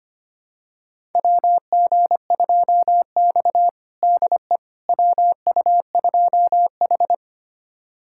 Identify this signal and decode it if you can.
Morse code, heard as a keyed tone, about 25 wpm, 700 Hz. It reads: WG2X DE WU2H
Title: USA Callsign DE Callsign - 4 Characters - 25wpm